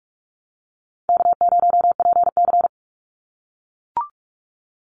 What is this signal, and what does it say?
Morse code, heard as a keyed tone, about 44 wpm, 700 Hz. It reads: K0PC
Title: Easy Call Signs - 45wpm